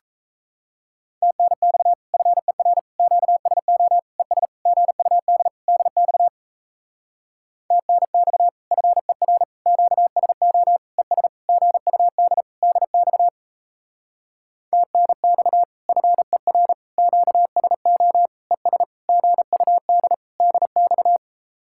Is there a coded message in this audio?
Yes: Morse code from 1.2 to 21.2 s, heard as a keyed tone, about 36 wpm, 700 Hz. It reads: TNX FER QSO ES GUD DX TNX FER QSO ES GUD DX TNX FER QSO ES GUD DX